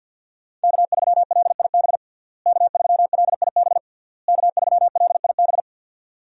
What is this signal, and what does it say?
Morse code, heard as a keyed tone, about 53 wpm, 700 Hz. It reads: K3LIB K3LIB K3LIB
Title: USA Call Signs - 5 Characters - Repeated 3x - 50wpm